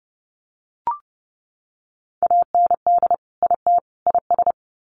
Morse code, heard as a keyed tone, about 30 wpm, 700 Hz. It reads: AND IT IS
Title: Sets of 3 Words 30wpm